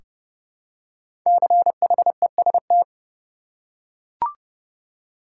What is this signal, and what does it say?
Morse code, heard as a keyed tone, about 30 wpm, 700 Hz. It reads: CHEST
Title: Five Letter Words - 30wpm